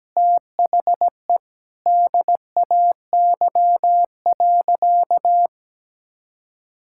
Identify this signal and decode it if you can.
Morse code, heard as a keyed tone, about 17 wpm, 700 Hz. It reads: THE DAY.
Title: Sentences from Top 1000 Words - Review Part 6 - 17wpm